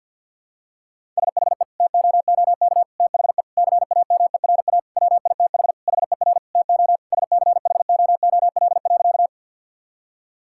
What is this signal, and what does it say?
Morse code, heard as a keyed tone, about 50 wpm, 700 Hz. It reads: SHE TOOK THE CAMERA WITH HER TO SCHOOL.
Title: Sentences from Top 1000 Words - Review Part 5 - 50wpm